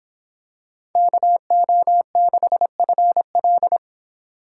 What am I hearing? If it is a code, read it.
Morse code, heard as a keyed tone, about 26 wpm, 700 Hz. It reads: KO6FL